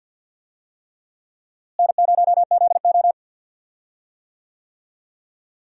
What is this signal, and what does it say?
Morse code, heard as a keyed tone, about 50 wpm, 700 Hz. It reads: N0ZO
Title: USA Call Signs - 4 Characters - Repeated 3x - 50wpm